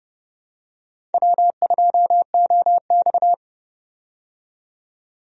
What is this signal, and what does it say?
Morse code, heard as a keyed tone, about 31 wpm, 700 Hz. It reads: W2OX
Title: USA Call Signs - 4 Characters - 30wpm